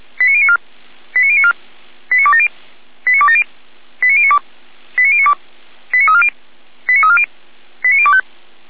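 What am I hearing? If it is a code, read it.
Five-tone selective call (ZVEI), heard as quick runs of five tones, tones of about 80 ms. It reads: 89094 89094 89270 89270 89092 89092 89340 89340 89025
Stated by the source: ZVEI five-tone selective call 89262, received by SDR, feuerwehr-orlamuende.org fire brigade.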